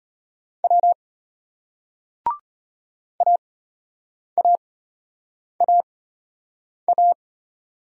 Morse code, heard as a keyed tone, about 34 wpm, spaced as for 8 wpm, 700 Hz. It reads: W AAAA